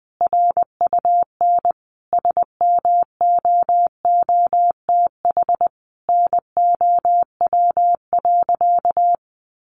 Morse code, heard as a keyed tone, about 20 wpm, 700 Hz. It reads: RUN SMOOTH NOW.
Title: Sentences from Top 1400 Words 20wpm